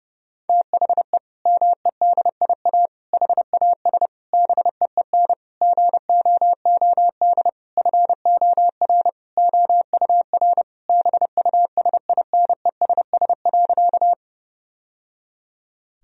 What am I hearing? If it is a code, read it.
Morse code, heard as a keyed tone, about 30 wpm, 700 Hz. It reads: THE MEDIA HAS BEEN GOOD FOR OUR BUSINESS.
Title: Sentences from Top 1000 Words - Review Part 4 - 30wpm